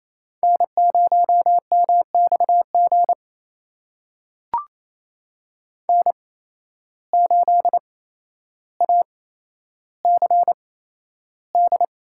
Morse code, heard as a keyed tone, about 28 wpm, spaced as for 12 wpm, 700 Hz. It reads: N0MXG N8ACD
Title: USA Call Signs - 5 Characters - 8x Character Spacing - 28wpm